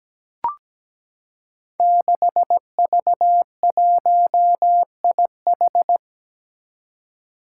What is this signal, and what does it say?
Morse code, heard as a keyed tone, about 17 wpm, 700 Hz. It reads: 6V1IH
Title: Five Challenging Letter-Number Combinations 17wpm